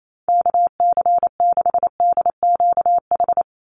Morse code, heard as a keyed tone, about 28 wpm, 700 Hz. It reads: KC6DQH
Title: USA Call Signs 28wpm